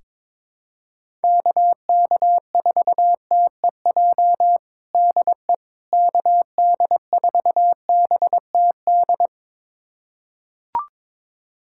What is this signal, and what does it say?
Morse code, heard as a keyed tone, about 22 wpm, 700 Hz. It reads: KK4TEJ DE KD4BTD
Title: USA Callsign DE Callsign - 6 Characters - 22wpm